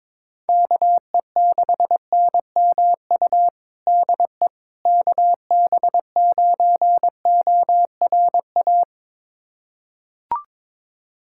Morse code, heard as a keyed tone, about 22 wpm, 700 Hz. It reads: KE6NMU DE KB9ORA
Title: USA Callsign DE Callsign 22wpm